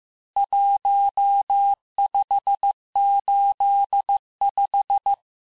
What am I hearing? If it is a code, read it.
Morse code, heard as a keyed tone, about 15 wpm, 800 Hz. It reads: 1585
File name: cours-du-lundi files / cours-1585-texte-chiffre-vitesse-14mpm